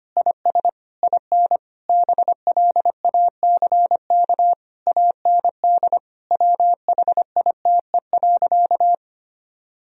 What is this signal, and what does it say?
Morse code, heard as a keyed tone, about 25 wpm, 700 Hz. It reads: IS IN BLACK AND WHITE.
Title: Sentences from Top 1000 Words - Review Part 2 - 25wpm